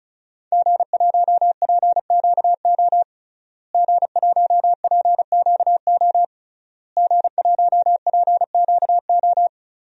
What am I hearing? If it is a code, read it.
Morse code, heard as a keyed tone, about 34 wpm, 700 Hz. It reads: G1PQO G1PQO G1PQO
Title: DXCC-100 Call Signs - 5 Characters - Repeated 3x - 35wpm